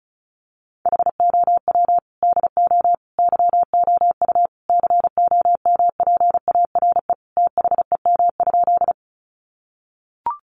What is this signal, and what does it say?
Morse code, heard as a keyed tone, about 34 wpm, 700 Hz. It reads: HOW DO YOU COMPARE THEM?
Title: Sentences from Top 1300 Words 35wpm